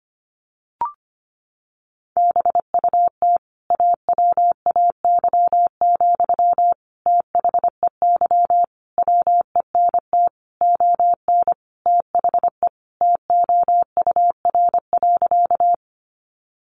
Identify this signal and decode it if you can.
Morse code, heard as a keyed tone, about 25 wpm, 700 Hz. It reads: BUT AWAY, THEY WENT ON THE TOUR.
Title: Sentences from Top 1400 Words 25wpm